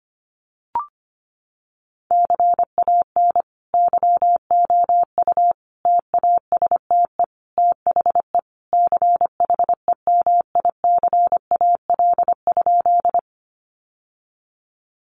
Morse code, heard as a keyed tone, about 25 wpm, 700 Hz. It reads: CAN YOU TASTE THE CHEMICAL?
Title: Sentences from Top 1200 Words 25wpm